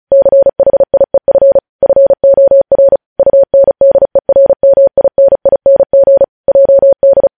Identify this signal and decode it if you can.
Morse code, heard as a keyed tone, about 36 wpm, 550 Hz. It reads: CHIEF FOR UNDERMINING JD
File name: podcast / News headlines-1x-35x35